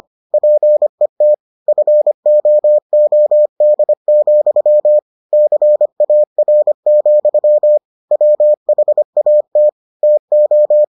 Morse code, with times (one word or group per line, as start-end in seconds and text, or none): 0.3-1.3 PET
1.7-5.0 FOOD,
5.3-7.8 CAR,
8.1-9.7 WHAT
10.0-10.8 TO